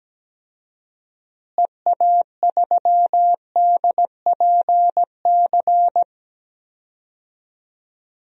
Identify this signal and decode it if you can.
Morse code, heard as a keyed tone, about 17 wpm, 700 Hz. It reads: EA3DPC
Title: DXCC-100 Call Signs - 6 Characters - Repeated 3x - 17wpm